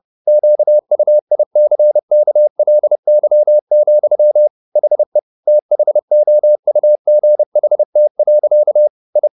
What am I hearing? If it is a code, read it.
Morse code, heard as a keyed tone, about 30 wpm, 600 Hz. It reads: QUICKLY, HE THOUGHT. I